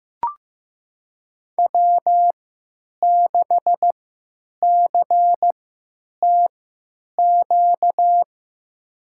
Morse code, heard as a keyed tone, about 16 wpm, spaced as for 9 wpm, 700 Hz. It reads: W6CTQ
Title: USA Call Signs - 5 Characters - 3x Character Spacing - 15wpm